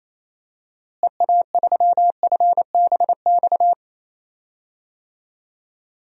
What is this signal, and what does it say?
Morse code, heard as a keyed tone, about 28 wpm, 700 Hz. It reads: EA3FBX
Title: DXCC-100 Call Signs - 6 Characters - Repeated 2x - 28wpm